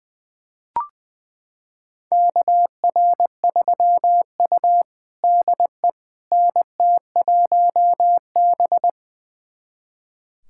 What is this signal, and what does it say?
Morse code, heard as a keyed tone, about 21 wpm, 700 Hz. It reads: KR3U DE NT1B
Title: USA Callsign DE Callsign - 4 Characters - 20wpm